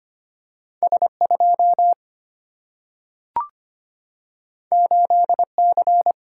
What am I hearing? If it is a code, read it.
Morse code, heard as a keyed tone, about 25 wpm, 700 Hz. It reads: S 2 8 C
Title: Single-Letter Single-Number 25wpm